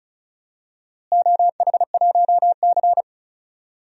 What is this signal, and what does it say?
Morse code, heard as a keyed tone, about 36 wpm, 700 Hz. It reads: OH1C
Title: DXCC-100 Call Signs - 4 Characters - Repeated 3x - 35wpm